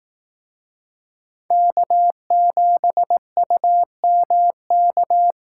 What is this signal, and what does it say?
Morse code, heard as a keyed tone, about 18 wpm, 700 Hz. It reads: K7UMK
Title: Speed Racing - USA 1x3 Call Signs 15wpm